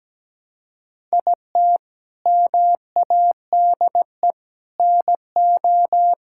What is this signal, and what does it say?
Morse code, heard as a keyed tone, about 17 wpm, 700 Hz. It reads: IT MADE NO
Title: Sentences from Top 1000 Words - Review Part 5 - 17wpm